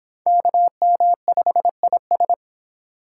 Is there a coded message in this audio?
Yes: Morse code from 0.3 to 2.3 s, heard as a keyed tone, about 23 wpm, 700 Hz. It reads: KM5IS